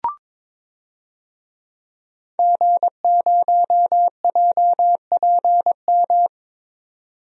Morse code, heard as a keyed tone, about 22 wpm, 700 Hz. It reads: G0JPM
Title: DXCC-100 Call Signs - 5 Characters - 22wpm